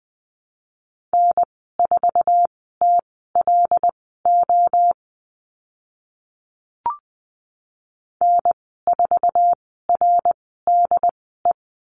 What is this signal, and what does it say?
Morse code, heard as a keyed tone, about 20 wpm, spaced as for 14 wpm, 700 Hz. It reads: N4TLO N4RDE
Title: 076 Lesson 13g - Letter L - Call Signs 20wpm